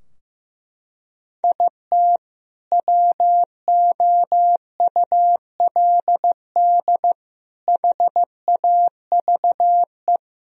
Morse code, heard as a keyed tone, about 15 wpm, 700 Hz. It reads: IT WOULD HAVE
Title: Sentences from Top 1000 Words - Review Part 4 - 15wpm